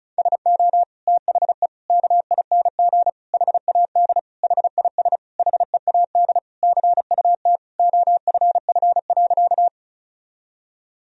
Morse code, heard as a keyed tone, about 34 wpm, 700 Hz. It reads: SO THE KING HAD HIS HEAD CUT OFF.
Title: Sentences from Top 1000 Words - Review Part 1 - 35wpm